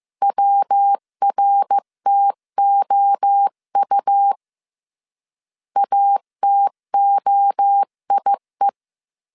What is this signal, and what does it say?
Morse code, heard as a keyed tone, about 15 wpm, 800 Hz. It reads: WRTOU ATOIE
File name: CW / DET / 1 3 1500C S C 14wpm 2458